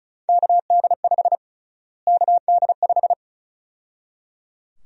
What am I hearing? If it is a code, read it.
Morse code, heard as a keyed tone, about 34 wpm, 700 Hz. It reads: KD5 KD5
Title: USA Call Sign Prefixes - Repeated 2x - 35wpm